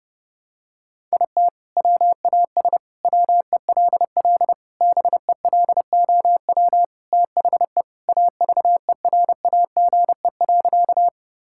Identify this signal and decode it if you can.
Morse code, heard as a keyed tone, about 30 wpm, 700 Hz. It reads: IT WAS WELL BELOW THE AVERAGE.
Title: Sentences from Top 1000 Words - Review Part 6 - 30wpm